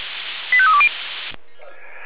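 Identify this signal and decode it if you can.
Five-tone selective call (ZVEI), heard as quick runs of five tones, tones of about 70 ms. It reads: 85320
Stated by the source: ZVEI five-tone call 85310 received by SDR, channel 496.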